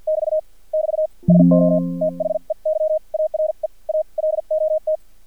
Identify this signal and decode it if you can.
Morse code, heard as a keyed tone, about 50 wpm, spaced as for 42 wpm, 620 Hz. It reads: <BT> <BT> SO THEY ARE A LOT
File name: archived / CRSnet